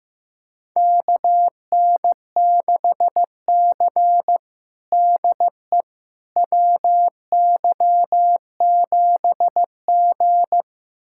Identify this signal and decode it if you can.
Morse code, heard as a keyed tone, about 15 wpm, 700 Hz. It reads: KN6C DE WY7G
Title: USA Callsign DE Callsign - 4 Characters - 15wpm